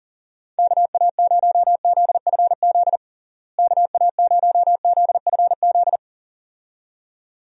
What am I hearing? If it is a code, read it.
Morse code, heard as a keyed tone, about 40 wpm, 700 Hz. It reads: KA0ZFZ KA0ZFZ
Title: USA Call Signs - 6 Characters - Repeated 2x - 40wpm